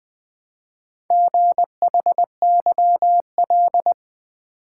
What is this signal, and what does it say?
Morse code, heard as a keyed tone, about 20 wpm, 700 Hz. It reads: GHYL
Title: Four Letter-Number Combinations - 20wpm